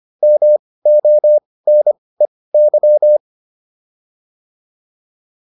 Morse code, heard as a keyed tone, about 25 wpm, spaced as for 18 wpm, 600 Hz. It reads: MONEY